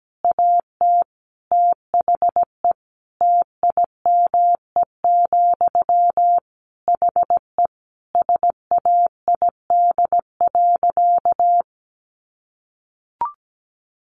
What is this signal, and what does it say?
Morse code, heard as a keyed tone, about 17 wpm, 700 Hz. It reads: AT THE TIME, HE SAID.
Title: Sentences from Top 1200 Words 17wpm